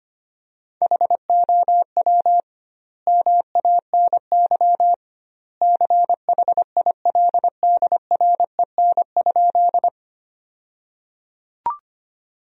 Morse code, heard as a keyed tone, about 25 wpm, 700 Hz. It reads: HOW MANY CHILDREN?
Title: 2x Word-Spacing - Sentences from Top 300 Words - 25wpm